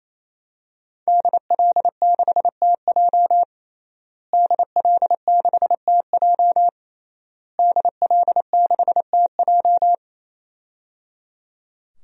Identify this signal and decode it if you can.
Morse code, heard as a keyed tone, about 28 wpm, 700 Hz. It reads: DL6TJ DL6TJ DL6TJ
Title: DXCC-100 Call Signs - 5 Characters - Repeated 3x - 28wpm